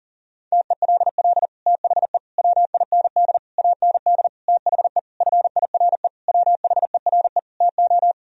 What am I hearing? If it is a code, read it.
Morse code, heard as a keyed tone, about 40 wpm, 700 Hz. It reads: TELL THE WIND AND THE FIRE WHERE TO
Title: Sentences from Top 1000 Words - Review Part 2 - 40wpm